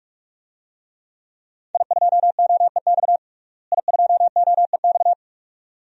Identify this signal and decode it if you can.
Morse code, heard as a keyed tone, about 42 wpm, 700 Hz. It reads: I2OEX I2OEX
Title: DXCC-100 Call Signs - 5 Characters - Repeated 2x - 45wpm